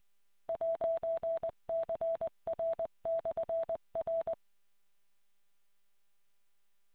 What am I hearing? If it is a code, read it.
Morse code, heard as a keyed tone, about 21 wpm, 660 Hz. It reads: 'CR/R